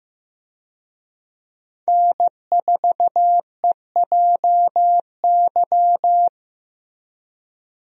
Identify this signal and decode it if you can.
Morse code, heard as a keyed tone, about 15 wpm, 700 Hz. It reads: N4EJY
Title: USA Call Signs - 5 Characters - Repeated 3x - 15wpm